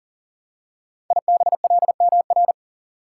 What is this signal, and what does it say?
Morse code, heard as a keyed tone, about 40 wpm, 700 Hz. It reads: IBLMR
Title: Code Groups - 5 Letters - 40wpm